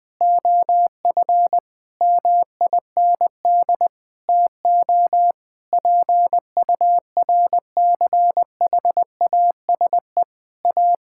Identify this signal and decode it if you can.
Morse code, heard as a keyed tone, about 21 wpm, 700 Hz. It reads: OF MIND TO PURCHASE A
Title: Sentences from Top 1000 Words - Review Part 6 - 20wpm